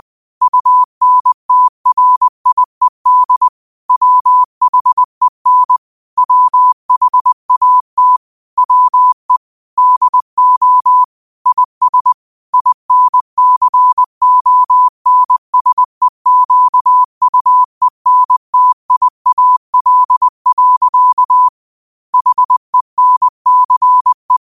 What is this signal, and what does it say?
Morse code, heard as a keyed tone, about 20 wpm, 1000 Hz. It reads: UNTRIED WHEN WHAT WE DO IS INCONSEQUENTIAL. HENCE